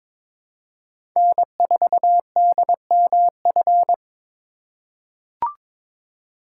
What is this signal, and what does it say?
Morse code, heard as a keyed tone, about 22 wpm, 700 Hz. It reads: N4DMF
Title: USA Call Signs - 5 Characters - 3x Character Spacing - 22wpm